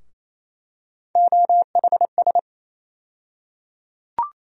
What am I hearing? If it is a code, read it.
Morse code, heard as a keyed tone, about 28 wpm, 700 Hz. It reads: OHS
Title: USA Call Sign Postfixes - 3 Letters 28wpm